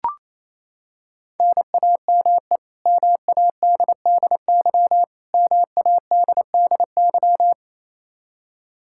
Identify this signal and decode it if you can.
Morse code, heard as a keyed tone, about 28 wpm, 700 Hz. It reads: NAME MADDY MADDY
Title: Practice QSO - Exchange 1 - Beginning - 28wpm